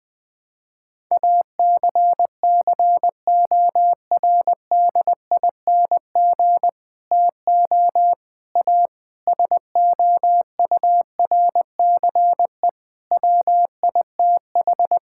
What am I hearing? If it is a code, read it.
Morse code, heard as a keyed tone, about 20 wpm, 700 Hz. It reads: ACCORDING TO A SOURCE WITH
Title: Sets of 9 Words - 20wpm